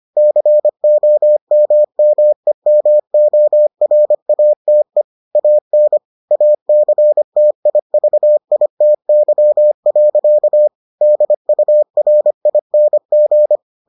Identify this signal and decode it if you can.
Morse code, heard as a keyed tone, about 25 wpm, 600 Hz. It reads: COMMEMORATE AN ACTIVITY. DURING